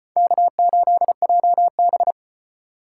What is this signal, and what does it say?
Morse code, heard as a keyed tone, about 30 wpm, 700 Hz. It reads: K8JB
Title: Speed Racing - USA Call Signs 25wpm